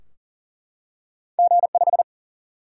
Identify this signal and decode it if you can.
Morse code, heard as a keyed tone, about 41 wpm, 700 Hz. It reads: G5
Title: Single-Letter Single-Number 40wpm